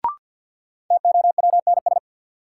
Morse code, heard as a keyed tone, about 50 wpm, 700 Hz. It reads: TOWNS
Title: Sentences from Top 1000 Words - Review Part 6 - New Word Drill - 50wpm